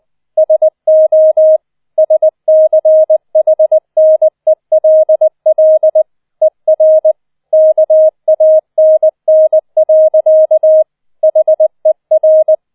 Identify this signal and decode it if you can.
Morse code, heard as a keyed tone, about 20 wpm, 630 Hz. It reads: SO SCHNELL ER KANN. HER